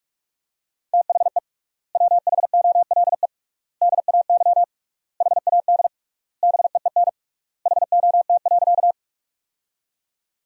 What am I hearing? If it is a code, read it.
Morse code, heard as a keyed tone, about 46 wpm, 700 Hz. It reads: THE WHOLE DAY HAD BEEN HOT.
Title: 3x Word-Spacing - Sentences from Top 500 Words - 45wpm